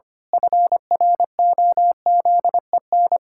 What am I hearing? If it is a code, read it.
Morse code, heard as a keyed tone, about 25 wpm, 700 Hz. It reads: FROZEN
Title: Pangram Challenge v2 - 25wpm